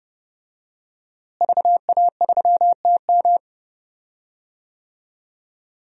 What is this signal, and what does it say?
Morse code, heard as a keyed tone, about 30 wpm, 700 Hz. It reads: VA3TM
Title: DXCC-100 Call Signs - 5 Characters - Repeated 2x - 30wpm